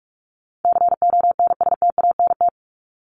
Morse code, heard as a keyed tone, about 44 wpm, 700 Hz. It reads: CONSTANT
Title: LY Suffix - Top 100 Words with Base Words 45wpm